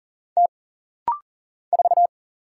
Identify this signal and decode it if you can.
Morse code, heard as a keyed tone, about 40 wpm, 700 Hz. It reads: T 4
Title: Single Letter-Number Alt - Rapid-Fire 40wpm